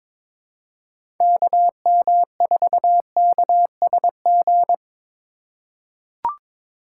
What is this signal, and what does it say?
Morse code, heard as a keyed tone, about 22 wpm, 700 Hz. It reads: KM4KSG
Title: USA Call Signs - 3x Character Spacing - 22wpm